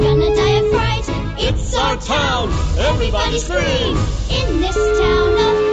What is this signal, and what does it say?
Ringback tone (Europe, 425 Hz), with speech.